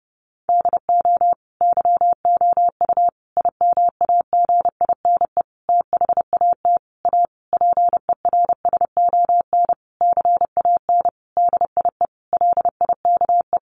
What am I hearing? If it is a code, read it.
Morse code, heard as a keyed tone, about 30 wpm, 700 Hz. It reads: DO YOU IMAGINE THAT A PERSON CAN DIE LIKE
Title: Sentences from Top 1400 Words 30wpm